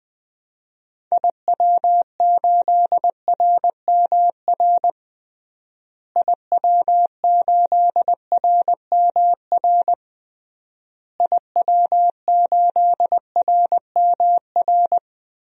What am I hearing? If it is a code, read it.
Morse code, heard as a keyed tone, about 20 wpm, 700 Hz. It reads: IW8RMR IW8RMR IW8RMR
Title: DXCC-100 Call Signs - 6 Characters - Repeated 3x - 20wpm